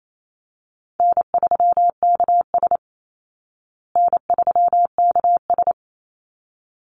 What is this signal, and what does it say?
Morse code, heard as a keyed tone, about 28 wpm, 700 Hz. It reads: N3KS N3KS
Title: Call Signs Repeated 3x - 28wpm